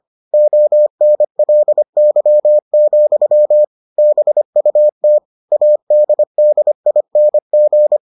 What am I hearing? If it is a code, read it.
Morse code, heard as a keyed tone, about 25 wpm, 600 Hz. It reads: ONLY, BUT ADDING